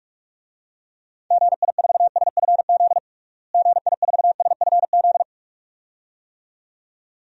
Morse code, heard as a keyed tone, about 46 wpm, 700 Hz. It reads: GI4SFZ GI4SFZ
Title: DXCC-100 Call Signs - 6 Characters - Repeated 2x - 45wpm